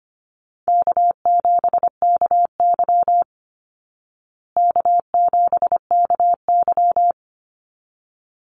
Morse code, heard as a keyed tone, about 25 wpm, 700 Hz. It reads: K7KY K7KY
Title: Call Signs Repeated 3x - 25wpm